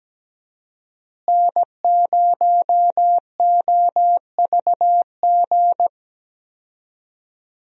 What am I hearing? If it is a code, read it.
Morse code, heard as a keyed tone, about 17 wpm, 700 Hz. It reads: N0OVG
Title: USA Call Signs - 5 Characters - Repeated 3x - 17wpm